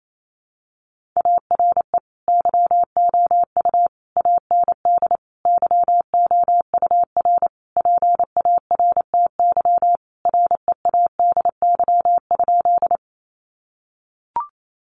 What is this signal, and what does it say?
Morse code, heard as a keyed tone, about 28 wpm, 700 Hz. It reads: ARE YOU AND YOUR PARTY READY?
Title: Sentences from Top 1400 Words 28wpm